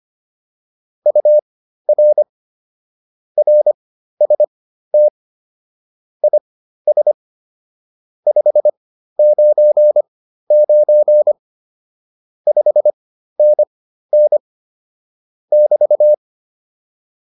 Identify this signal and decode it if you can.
Morse code, heard as a keyed tone, about 25 wpm, spaced as for 13 wpm, 600 Hz. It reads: UR RST IS 599 5NN <BT>